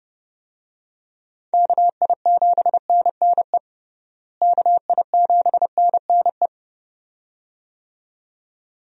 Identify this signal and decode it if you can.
Morse code, heard as a keyed tone, about 29 wpm, 700 Hz. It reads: KI7NNE KI7NNE
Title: USA Call Signs - 6 Characters - Repeated 2x - 30wpm